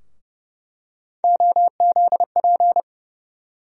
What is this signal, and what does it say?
Morse code, heard as a keyed tone, about 31 wpm, 700 Hz. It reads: OZP
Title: USA Call Sign Postfixes - 3 Letters 30wpm